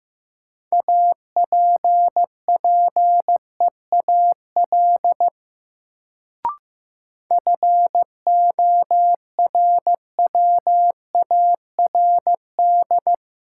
Morse code, heard as a keyed tone, about 15 wpm, 700 Hz. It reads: APPEAL FORWARD
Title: Sentences from Top 1300 Words - New Word Drill Alt - 15wpm